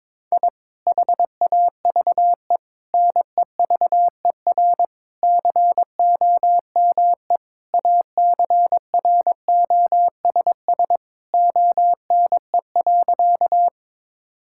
Morse code, heard as a keyed tone, about 22 wpm, 700 Hz. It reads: I HAVE NEVER COME ACROSS ONE.
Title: Sentences from Top 1000 Words - Review Part 5 - 22wpm